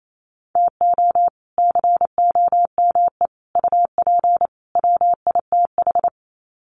Morse code, heard as a keyed tone, about 28 wpm, 700 Hz. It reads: TO COME UP WITH
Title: Sets of 4 Words 28wpm